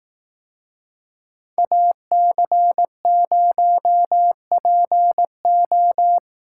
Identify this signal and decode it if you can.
Morse code, heard as a keyed tone, about 19 wpm, 700 Hz. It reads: AC0PO
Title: Speed Racing - USA 2x2 Call Signs 15wpm